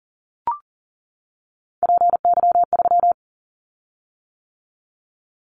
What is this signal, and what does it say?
Morse code, heard as a keyed tone, about 38 wpm, 700 Hz. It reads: PY3
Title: Callsign Prefixes for Top 100 DXCC - 40wpm